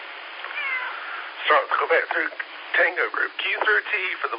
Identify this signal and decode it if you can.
Emergency Alert System (SAME) tones, heard as harsh buzzing bursts, with speech. It is heard from 0.6 to 4.1 s.
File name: SimplexLogic